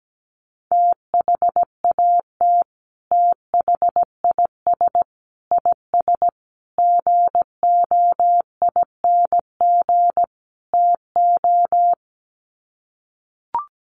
Morse code, heard as a keyed tone, about 17 wpm, 700 Hz. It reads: THAT THIS IS GOING TO
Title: Sets of 5 Words Encore2 17wpm